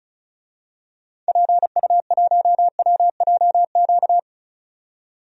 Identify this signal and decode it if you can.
Morse code, heard as a keyed tone, about 36 wpm, 700 Hz. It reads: PU1WJQ
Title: DXCC-100 Call Signs - 6 Characters - 35wpm